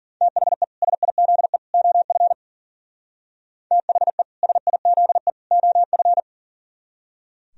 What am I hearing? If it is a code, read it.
Morse code, heard as a keyed tone, about 50 wpm, 700 Hz. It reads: THE SIZE OF THE SIZE OF